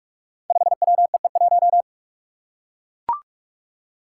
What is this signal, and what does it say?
Morse code, heard as a keyed tone, about 44 wpm, 700 Hz. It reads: 5WEE1
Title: Five Challenging Letter-Number Combinations 45wpm